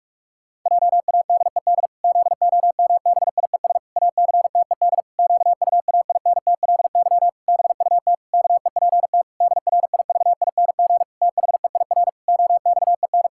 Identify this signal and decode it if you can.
Morse code, heard as a keyed tone, about 41 wpm, 700 Hz. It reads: JADED ZOMBIES ACTED QUAINTLY BUT KEPT DRIVING THEIR OXEN
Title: Pangram Challenge v2 - 45wpm